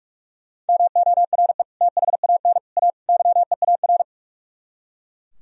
Morse code, heard as a keyed tone, about 51 wpm, 700 Hz. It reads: MORE THAN A YEAR
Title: Speed-Racing - Sets of 4 Words 45wpm